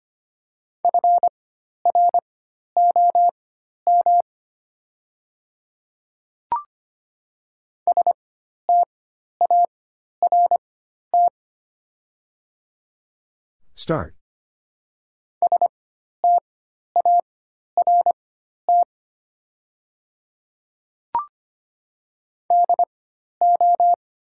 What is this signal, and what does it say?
Morse code, heard as a keyed tone, about 25 wpm, spaced as for 12 wpm, 700 Hz. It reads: FROM START START DO
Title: 4x Character Spacing - Top 500 Words - 25wpm